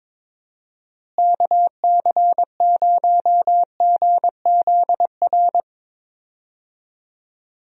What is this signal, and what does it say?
Morse code, heard as a keyed tone, about 22 wpm, 700 Hz. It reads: KC0GZR
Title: USA Call Signs - 6 Characters - Repeated 2x - 22wpm